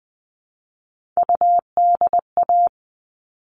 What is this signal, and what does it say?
Morse code, heard as a keyed tone, about 20 wpm, 700 Hz. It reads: UDA
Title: All Cut Numbers - 3-Digit 20wpm